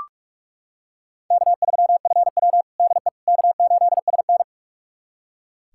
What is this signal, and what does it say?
Morse code, heard as a keyed tone, about 45 wpm, 700 Hz. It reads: K3FW DE K8SN
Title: USA Callsign DE Callsign - 4 Characters - 45wpm